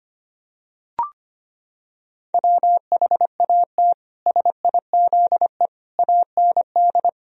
Morse code, heard as a keyed tone, about 25 wpm, 700 Hz. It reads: WHAT SIZE AND
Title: Sentences from Top 1000 Words - Review Part 3 - 25wpm